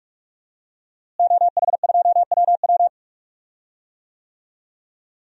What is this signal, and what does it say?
Morse code, heard as a keyed tone, about 43 wpm, 700 Hz. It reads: OH2WW
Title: DXCC-100 Call Signs - 5 Characters - Repeated 3x - 45wpm